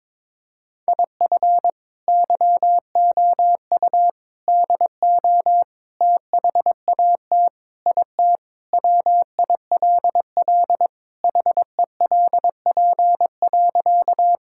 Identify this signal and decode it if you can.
Morse code, heard as a keyed tone, about 22 wpm, 700 Hz. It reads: IF YOU DO THAT IT WILL HELP.
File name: practice / 22wpm / Sentences with 3-Word Phrases - 22wpm